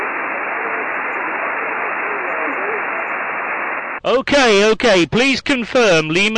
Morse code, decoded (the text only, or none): none